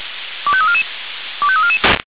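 Five-tone selective call (ZVEI), heard as quick runs of five tones, tones of about 70 ms. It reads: 25344 25344
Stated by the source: ZVEI five-tone call 25310 received on an SDR.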